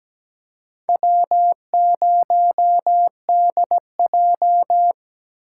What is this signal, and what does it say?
Morse code, heard as a keyed tone, about 20 wpm, 700 Hz. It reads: W0DJ
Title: Speed Racing - USA 1x2 Call Signs 17wpm